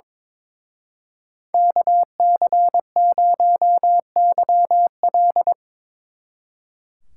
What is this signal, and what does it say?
Morse code, heard as a keyed tone, about 25 wpm, 700 Hz. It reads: KC0YL
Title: Speed Racing - USA 2x2 Call Signs 22wpm